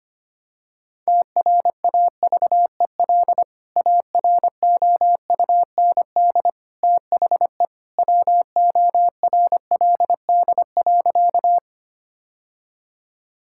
Morse code, heard as a keyed tone, about 25 wpm, 700 Hz. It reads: TRAVEL AROUND THE WORLD.
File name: practice / 25wpm / Sentences with 3-Word Phrases - 25wpm